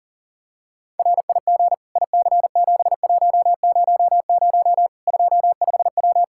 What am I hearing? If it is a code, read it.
Morse code, heard as a keyed tone, about 37 wpm, 700 Hz. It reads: RIG IC7100 25W
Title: Practice QSO - Exchange 2 - Info - 40wpm